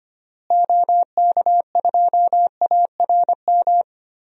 Morse code, heard as a keyed tone, about 25 wpm, 700 Hz. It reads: OK2ARM
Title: DXCC-100 Call Signs - 6 Characters - Repeated 3x - 25wpm